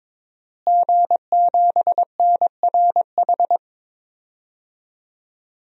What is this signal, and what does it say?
Morse code, heard as a keyed tone, about 21 wpm, 700 Hz. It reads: G7NRH
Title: DXCC-100 Call Signs - 5 Characters - Repeated 3x - 22wpm